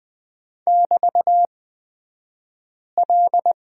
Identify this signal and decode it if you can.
Morse code, heard as a keyed tone, about 20 wpm, spaced as for 5 wpm, 700 Hz. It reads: <BT>L
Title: 242 Lesson 39d - Letter z - Two Characters 20wpm